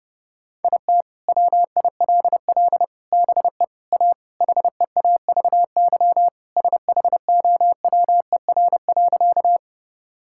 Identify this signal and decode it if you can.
Morse code, heard as a keyed tone, about 30 wpm, 700 Hz. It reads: IT WILL BE A HEAVY SHOWER.
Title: Sentences from Top 1600 Words 30wpm